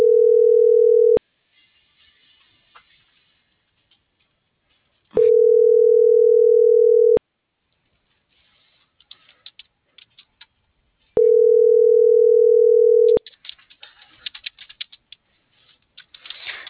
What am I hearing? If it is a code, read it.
Ringback tone (North America), 440 + 480 Hz.